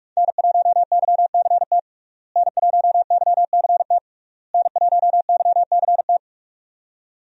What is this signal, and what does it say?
Morse code, heard as a keyed tone, about 46 wpm, 700 Hz. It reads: N1YCT N1YCT N1YCT
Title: USA Call Signs - 5 Characters - Repeated 3x - 45wpm